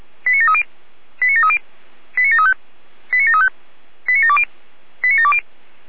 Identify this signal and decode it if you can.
Five-tone selective call (ZVEI), heard as quick runs of five tones, tones of about 70 ms. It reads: 89730 89730 89735 89735 89720 89720
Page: ZVEI